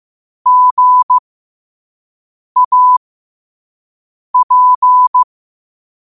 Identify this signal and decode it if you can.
Morse code, heard as a keyed tone, about 15 wpm, spaced as for 5 wpm, 1000 Hz. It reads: GAP
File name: qotdfiles / 5wpm